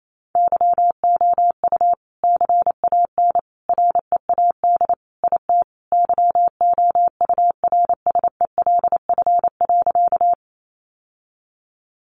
Morse code, heard as a keyed tone, about 28 wpm, 700 Hz. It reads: YOU CAN READ IT YOURSELF.
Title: Sentences from Top 600 Words 28wpm